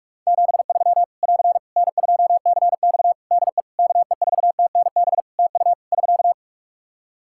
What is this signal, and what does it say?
Morse code, heard as a keyed tone, about 46 wpm, 700 Hz. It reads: 73 <AR> N2CX DE KE4TNB TU <SK>
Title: Practice QSO - Exchange 4 - Ending - 45wpm